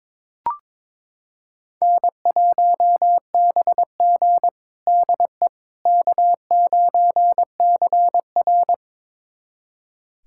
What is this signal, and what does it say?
Morse code, heard as a keyed tone, about 22 wpm, 700 Hz. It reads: N1BG DE K9CR
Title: USA Callsign DE Callsign - 4 Characters - 22wpm